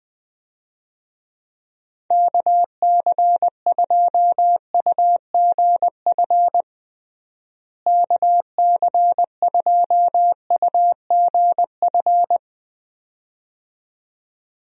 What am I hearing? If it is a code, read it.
Morse code, heard as a keyed tone, about 21 wpm, 700 Hz. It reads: KC2UGF KC2UGF
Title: USA Call Signs - 6 Characters - Repeated 2x - 20wpm